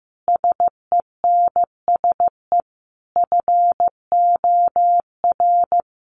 Morse code, heard as a keyed tone, about 15 wpm, 700 Hz. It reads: SENSE FOR
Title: Sentences from Top 700 Words 15wpm